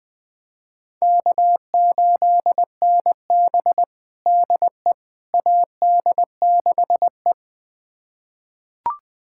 Morse code, heard as a keyed tone, about 20 wpm, 700 Hz. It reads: K8NB DE AD6E
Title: USA Callsign DE Callsign - 4 Characters - 20wpm